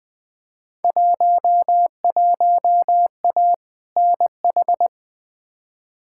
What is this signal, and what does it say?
Morse code, heard as a keyed tone, about 20 wpm, 700 Hz. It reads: 11A NH
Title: Field Day Exchanges Encore - 20wpm